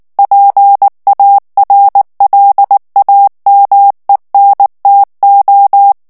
Morse code, heard as a keyed tone, about 19 wpm, 800 Hz. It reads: PARLAMENTO
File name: morse mp3 / ESPAGNOL TEXTE EN CLAIR 2 VITESSE 18 MOTS MINUTES